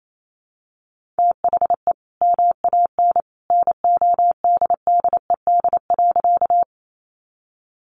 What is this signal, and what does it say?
Morse code, heard as a keyed tone, about 28 wpm, 700 Hz. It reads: THE MAN NODDED.
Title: A Sound Of Thunder 28wpm